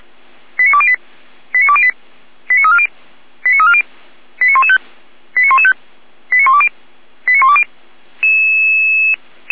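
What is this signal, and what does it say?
Five-tone selective call (ZVEI), heard as quick runs of five tones, tones of about 80 ms. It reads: 89208 89208 89340 89340 89115 89115 89120 89120
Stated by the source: ZVEI five-tone selective call 89094, received on SDR.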